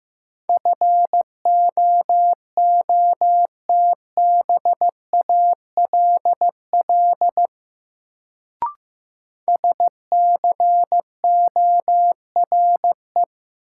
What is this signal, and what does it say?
Morse code, heard as a keyed tone, about 15 wpm, 700 Hz. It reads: FOOTBALL SCORE
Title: Sentences from Top 1600 Words - New Word Drill Alt - 15wpm